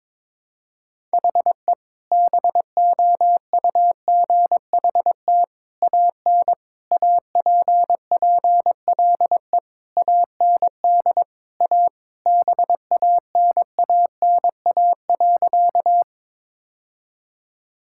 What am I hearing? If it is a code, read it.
Morse code, heard as a keyed tone, about 22 wpm, 700 Hz. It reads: HE BOUGHT AN APPLE AND A BANANA.
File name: practice / 22wpm / Sentences with 2-Word Phrases - 22wpm